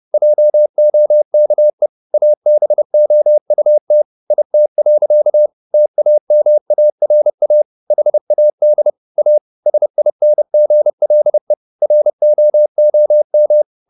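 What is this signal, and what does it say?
Morse code, heard as a keyed tone, about 30 wpm, 600 Hz. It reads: JOKE ABOUT IT. TAMARA HAD A SINGLE ROOM
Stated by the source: Morse code at 30 wpm, the novel Never by Ken Follett in English, from CW med Gnister.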